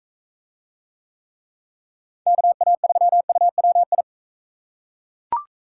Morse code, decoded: KA3UWI

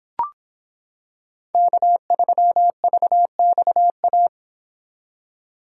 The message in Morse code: K3VXA

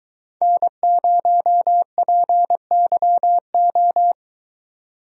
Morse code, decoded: N0PYO

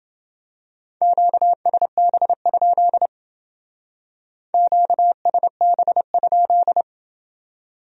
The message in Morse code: QSB? QSB?

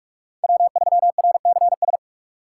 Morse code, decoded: W3RCS

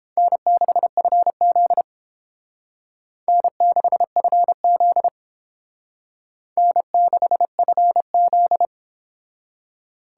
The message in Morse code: N6FZ N6FZ N6FZ

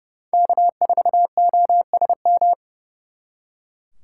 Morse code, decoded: K4OSM

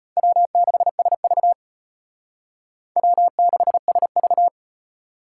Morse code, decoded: W6SV W6SV

